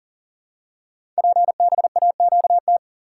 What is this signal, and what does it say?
Morse code, heard as a keyed tone, about 42 wpm, 700 Hz. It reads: PBAQT
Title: Code Groups - 5 Letters - 40wpm